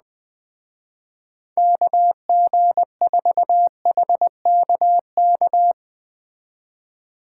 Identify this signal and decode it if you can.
Morse code, heard as a keyed tone, about 20 wpm, 700 Hz. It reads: KG4HKK
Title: Speed Racing - USA 2x3 Call Signs 17wpm